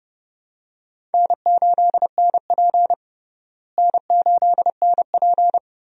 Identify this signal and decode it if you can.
Morse code, heard as a keyed tone, about 32 wpm, 700 Hz. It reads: N8NP N8NP
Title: USA Call Signs - 4 Characters - Repeated 3x - 30wpm